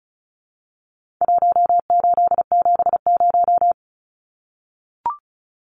Morse code, heard as a keyed tone, about 34 wpm, 700 Hz. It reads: 1870
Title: Numbers - Contest Style - 4-Digit 35wpm